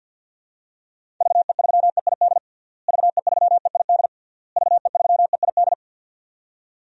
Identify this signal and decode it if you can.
Morse code, heard as a keyed tone, about 46 wpm, 700 Hz. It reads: VE3EID VE3EID VE3EID
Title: DXCC-100 Call Signs - 6 Characters - Repeated 3x - 50wpm